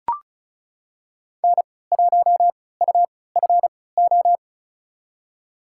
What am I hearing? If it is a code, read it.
Morse code, heard as a keyed tone, about 36 wpm, spaced as for 20 wpm, 700 Hz. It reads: N1UFO
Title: USA Call Signs - 3x Character Spacing - 35wpm